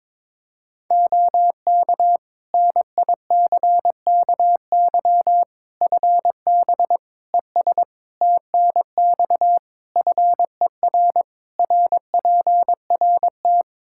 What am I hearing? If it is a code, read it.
Morse code, heard as a keyed tone, about 21 wpm, 700 Hz. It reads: OK NICKY FB ES TNX FER RPRT
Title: Practice QSO - Exchange 2 - Info - 22wpm